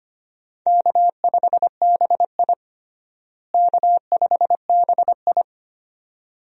Morse code, heard as a keyed tone, about 25 wpm, 700 Hz. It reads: K5BI K5BI
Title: USA Call Signs - 4 Characters - Repeated 2x - 25wpm